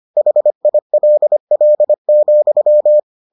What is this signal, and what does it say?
Morse code, heard as a keyed tone, about 25 wpm, 600 Hz. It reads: HILL,